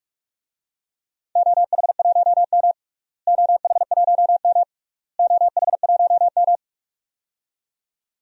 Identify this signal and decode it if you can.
Morse code, heard as a keyed tone, about 46 wpm, 700 Hz. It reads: OH1M OH1M OH1M
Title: DXCC-100 Call Signs - 4 Characters - Repeated 3x - 45wpm